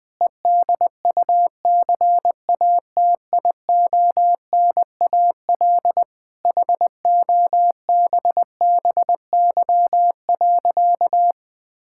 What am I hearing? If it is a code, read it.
Morse code, heard as a keyed tone, about 21 wpm, 700 Hz. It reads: EDUCATIONAL HOBBY.
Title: Difficult Sentences with Hints - 20wpm